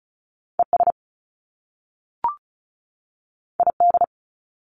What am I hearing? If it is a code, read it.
Morse code, heard as a keyed tone, about 36 wpm, 700 Hz. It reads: ES ID
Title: Top Two Letter Combinations in Words 35wpm